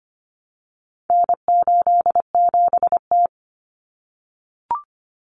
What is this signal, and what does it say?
Morse code, heard as a keyed tone, about 25 wpm, 700 Hz. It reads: N87T
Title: Common Cut Numbers - 4-Digit 25wpm